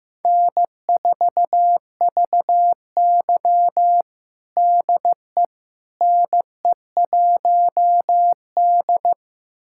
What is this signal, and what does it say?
Morse code, heard as a keyed tone, about 15 wpm, 700 Hz. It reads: N4VY DE NE1D
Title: USA Callsign DE Callsign - 4 Characters - 15wpm